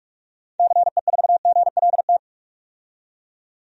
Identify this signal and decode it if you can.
Morse code, heard as a keyed tone, about 42 wpm, 700 Hz. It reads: CE4GLT
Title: DXCC-100 Call Signs - 6 Characters - 45wpm